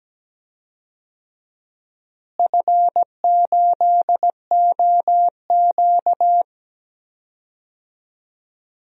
Morse code, heard as a keyed tone, about 17 wpm, 700 Hz. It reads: F8OQ
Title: DXCC-100 Call Signs - 4 Characters - Repeated 3x - 17wpm